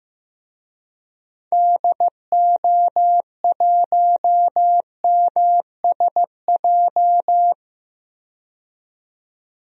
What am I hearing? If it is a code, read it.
Morse code, heard as a keyed tone, about 15 wpm, 700 Hz. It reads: DO1MSJ
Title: DXCC-100 Call Signs - 6 Characters - 15wpm